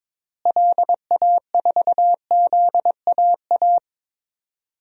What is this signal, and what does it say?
Morse code, heard as a keyed tone, about 22 wpm, 700 Hz. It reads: LA4ZAA
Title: DXCC-100 Call Signs - 6 Characters - Repeated 2x - 22wpm